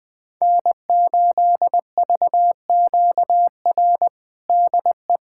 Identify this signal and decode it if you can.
Morse code, heard as a keyed tone, about 20 wpm, 700 Hz. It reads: N8VQR DE
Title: USA Callsign DE Callsign - 5 Characters - 20wpm